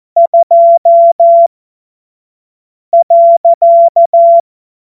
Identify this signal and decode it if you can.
Morse code, heard as a keyed tone, about 14 wpm, spaced as for 5 wpm, 680 Hz. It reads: 2.
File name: Morsekurs / Witze / Witz 25. April 2023 25 F70